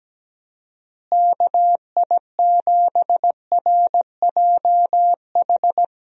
Morse code, heard as a keyed tone, about 17 wpm, 700 Hz. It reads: KI7RJH
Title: USA Call Signs - 6 Characters - Repeated 3x - 17wpm